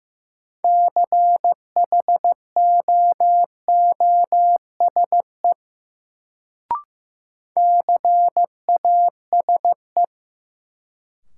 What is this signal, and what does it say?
Morse code, heard as a keyed tone, about 15 wpm, 700 Hz. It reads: CHOOSE CASE
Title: Sentences from Top 700 Words - New Word Drill - 15wpm